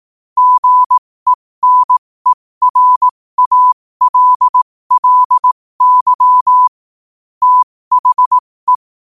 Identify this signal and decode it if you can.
Morse code, heard as a keyed tone, about 18 wpm, spaced as for 15 wpm, 1000 Hz. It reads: GENERALLY THE
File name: qotdfiles / 15wpm